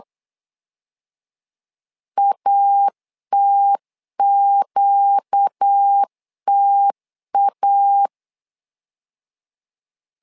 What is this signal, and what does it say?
Morse code, heard as a keyed tone, about 8 wpm, 800 Hz. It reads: ATQTA
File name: LEZIONI / 4 Q 8wpm 2732